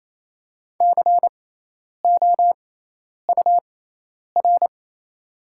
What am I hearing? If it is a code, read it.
Morse code, heard as a keyed tone, about 28 wpm, spaced as for 10 wpm, 700 Hz. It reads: COUR
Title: 6x Character Spacing - Top 500 Words - 28wpm